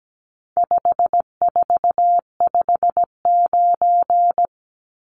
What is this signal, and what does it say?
Morse code, heard as a keyed tone, about 17 wpm, 700 Hz. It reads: 5459
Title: Numbers - 4-Digit 17wpm